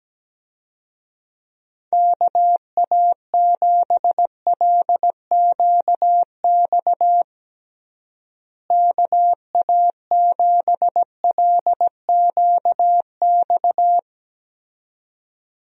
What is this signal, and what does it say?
Morse code, heard as a keyed tone, about 17 wpm, 700 Hz. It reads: KA7LQX KA7LQX
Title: USA Call Signs - 6 Characters - Repeated 2x - 17wpm